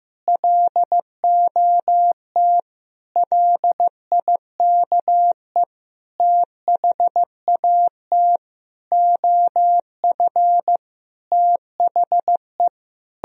Morse code, heard as a keyed tone, about 15 wpm, 700 Hz. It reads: LOT LIKE THAT OF THE
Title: Sentences from Top 1000 Words - Review Part 2 - 15wpm